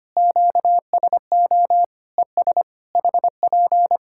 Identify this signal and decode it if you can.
Morse code, heard as a keyed tone, about 25 wpm, 700 Hz. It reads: QSO ES HP
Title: Practice QSO - Exchange 4 - Ending - 25wpm